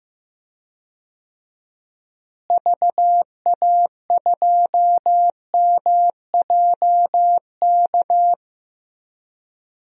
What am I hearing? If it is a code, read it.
Morse code, heard as a keyed tone, about 15 wpm, 700 Hz. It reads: VA2MJK
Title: DXCC-100 Call Signs - 6 Characters - Repeated 3x - 15wpm